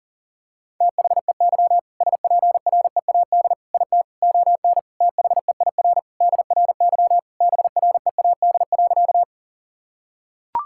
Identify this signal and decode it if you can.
Morse code, heard as a keyed tone, about 40 wpm, 700 Hz. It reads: THEY SPREAD IT ON THEIR DRY BREAD.
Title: Sentences from Top 1600 Words 40wpm